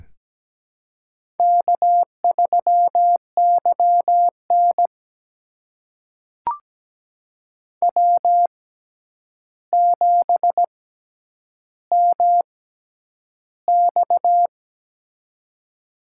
Morse code, heard as a keyed tone, about 17 wpm, spaced as for 9 wpm, 700 Hz. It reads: K3YN W7MX
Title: USA Call Signs - 4 Characters - 6x Character Spacing - 17wpm